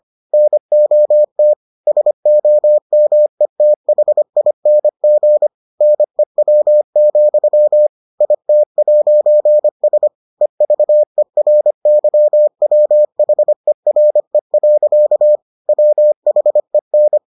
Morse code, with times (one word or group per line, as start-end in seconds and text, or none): 0.3-1.5 NOT
1.9-5.5 SOMETHING
5.8-7.9 NEW,
8.2-10.1 IT'S
10.4-15.4 EVERYWHERE.
15.7-17.2 WHEN